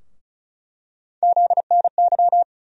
Morse code, heard as a keyed tone, about 36 wpm, 700 Hz. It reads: ZNY
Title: USA Call Sign Postfixes - 3 Letters 35wpm